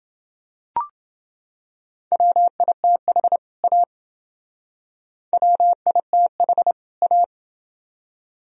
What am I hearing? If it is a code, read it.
Morse code, heard as a keyed tone, about 29 wpm, 700 Hz. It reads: WITH A WITH A